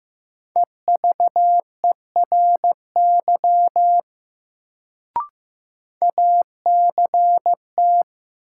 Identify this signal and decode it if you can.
Morse code, heard as a keyed tone, about 15 wpm, 700 Hz. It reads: EVERY ACT
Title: Sentences from Top 200 Words - New Word Drill - 15wpm